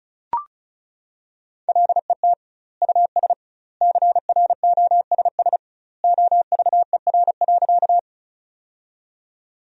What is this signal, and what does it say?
Morse code, heard as a keyed tone, about 34 wpm, 700 Hz. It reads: LET US CROSS OVER.
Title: 2x Word-Spacing - Sentences from Top 300 Words - 35wpm